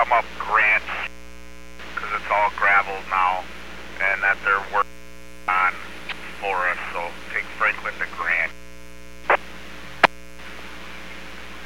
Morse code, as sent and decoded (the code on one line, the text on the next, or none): none